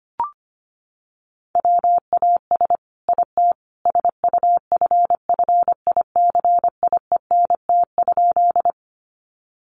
Morse code, heard as a keyed tone, about 25 wpm, 700 Hz. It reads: WAS IT SUFFICIENT?
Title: Sentences from Top 1400 Words 25wpm